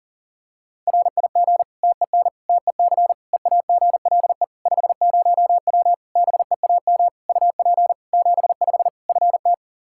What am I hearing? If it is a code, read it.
Morse code, heard as a keyed tone, about 40 wpm, 700 Hz. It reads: RIG TEN TEC EAGLE 50W BEAM UP 75 FT
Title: Practice QSO - Exchange 2 - Info - 40wpm